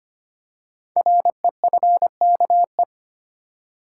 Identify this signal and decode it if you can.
Morse code, heard as a keyed tone, about 25 wpm, 700 Hz. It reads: REFKE
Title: Code Groups - 5 Letter:Numbers - 25wpm